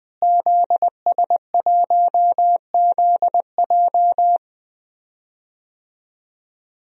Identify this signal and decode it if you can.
Morse code, heard as a keyed tone, about 20 wpm, 700 Hz. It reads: ZS1ZJ
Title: DXCC-100 Call Signs - 5 Characters - Repeated 3x - 20wpm